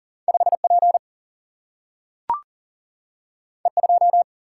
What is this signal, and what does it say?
Morse code, heard as a keyed tone, about 39 wpm, 700 Hz. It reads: 5 P E 2
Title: Single-Letter Single-Number 40wpm